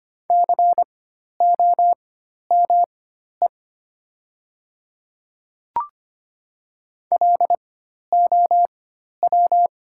Morse code, heard as a keyed tone, about 25 wpm, spaced as for 12 wpm, 690 Hz. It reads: COME LOW
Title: 4x Character Spacing - Top 200 Words - 25wpm